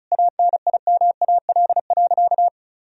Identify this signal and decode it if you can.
Morse code, heard as a keyed tone, about 36 wpm, 700 Hz. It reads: ANIMAL.
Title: Sentences from Top 1000 Words - Review Part 1 - 35wpm